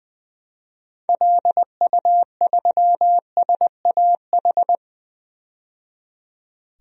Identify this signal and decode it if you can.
Morse code, heard as a keyed tone, about 20 wpm, 700 Hz. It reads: LU3SAH
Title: DXCC-100 Call Signs - 6 Characters - 20wpm